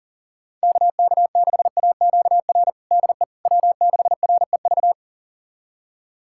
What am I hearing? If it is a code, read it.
Morse code, heard as a keyed tone, about 40 wpm, 700 Hz. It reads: KK6AQR DE W6REV
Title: USA Callsign DE Callsign 40wpm